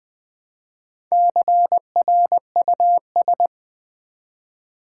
Morse code, heard as a keyed tone, about 20 wpm, 700 Hz. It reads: CRUS
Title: Build A Word - 1 Letter At A Time - 5-Letter Words - 20wpm